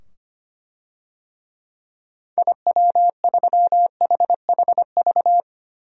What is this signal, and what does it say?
Morse code, heard as a keyed tone, about 25 wpm, 700 Hz. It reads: IW3HHV
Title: DXCC-100 Call Signs - 6 Characters - 25wpm